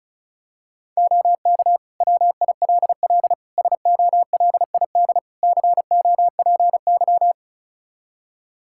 Morse code, heard as a keyed tone, about 34 wpm, 700 Hz. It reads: OK WILL SOLID COPY
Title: Practice QSO - Exchange 3 - More Info - 35wpm